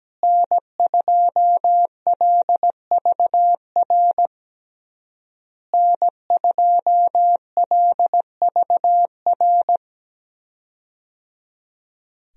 Morse code, heard as a keyed tone, about 17 wpm, 700 Hz. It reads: N2LVR N2LVR
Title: USA Call Signs - 5 Characters - Repeated 3x - 17wpm